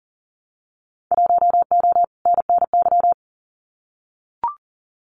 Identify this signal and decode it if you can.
Morse code, heard as a keyed tone, about 39 wpm, 700 Hz. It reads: 1O NNY
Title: Winter Field Day Exchanges 40wpm